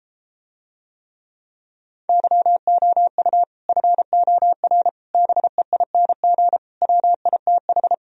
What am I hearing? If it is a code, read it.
Morse code, heard as a keyed tone, about 33 wpm, 700 Hz. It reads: YOU FOR BEING WITH